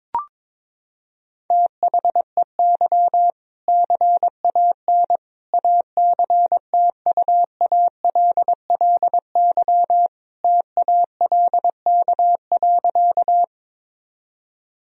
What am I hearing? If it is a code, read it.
Morse code, heard as a keyed tone, about 22 wpm, 700 Hz. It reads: THEY CAN ACTUALLY TALK.
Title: Sentences from Top 1000 Words - Review Part 4 - 22wpm